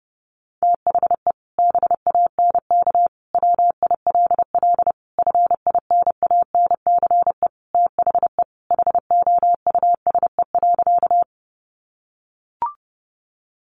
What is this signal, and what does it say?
Morse code, heard as a keyed tone, about 30 wpm, 700 Hz. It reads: THE BANK WILL FINANCE THE HOUSE.
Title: Sentences from Top 1500 Words 30wpm